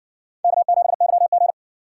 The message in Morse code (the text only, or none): K7YG